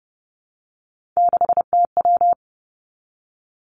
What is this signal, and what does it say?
Morse code, heard as a keyed tone, about 30 wpm, 700 Hz. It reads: 6TW
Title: Three Challenging Letter-Number Combinations 30wpm